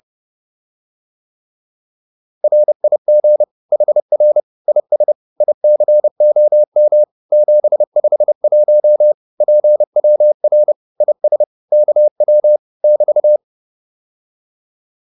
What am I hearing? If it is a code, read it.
Morse code, heard as a keyed tone, about 30 wpm, 600 Hz. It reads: RIG HR IS ICOM 751 PWR IS KW <BT>